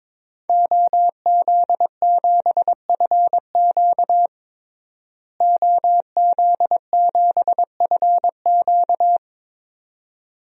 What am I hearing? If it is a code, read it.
Morse code, heard as a keyed tone, about 22 wpm, 700 Hz. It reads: OZ7FQ OZ7FQ
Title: DXCC-100 Call Signs - 5 Characters - Repeated 2x - 22wpm